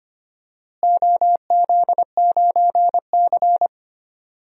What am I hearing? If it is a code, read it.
Morse code, heard as a keyed tone, about 25 wpm, 700 Hz. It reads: OZ9C
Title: DXCC-100 Call Signs - 4 Characters - Repeated 2x - 25wpm